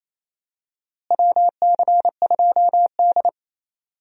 Morse code, heard as a keyed tone, about 27 wpm, 700 Hz. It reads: WC2D
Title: USA Call Signs - 4 Characters - Repeated 3x - 28wpm